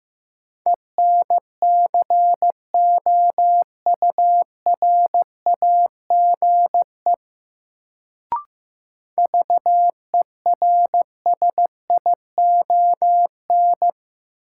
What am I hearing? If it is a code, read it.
Morse code, heard as a keyed tone, about 15 wpm, 700 Hz. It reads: ENCOURAGE VERSION
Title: Sentences from Top 1400 Words - New Word Drill - 15wpm